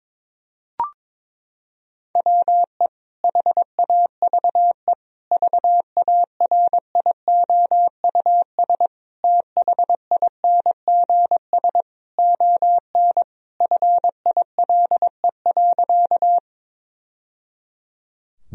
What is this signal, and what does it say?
Morse code, heard as a keyed tone, about 22 wpm, 700 Hz. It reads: WE HAVE VARIOUS THINGS ON FILE.
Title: Sentences from Top 1000 Words - Review Part 4 - 22wpm